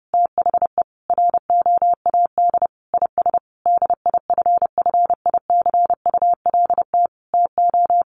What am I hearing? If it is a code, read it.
Morse code, heard as a keyed tone, about 30 wpm, 700 Hz. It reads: THE ROAD IS DIFFICULT TO
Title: Sentences from Top 1200 Words 30wpm